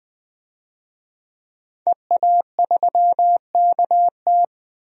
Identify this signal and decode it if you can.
Morse code, heard as a keyed tone, about 20 wpm, 700 Hz. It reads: EA3KT
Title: DXCC-100 Call Signs - 5 Characters - 20wpm